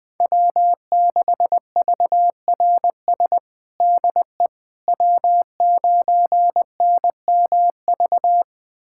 Morse code, heard as a keyed tone, about 20 wpm, 700 Hz. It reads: W6VRS DE W9NMV
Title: USA Callsign DE Callsign - 5 Characters - 20wpm